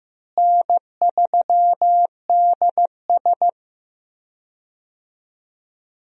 Morse code, heard as a keyed tone, about 15 wpm, 700 Hz. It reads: N3DS
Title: USA Call Signs - 4 Characters - Repeated 2x - 15wpm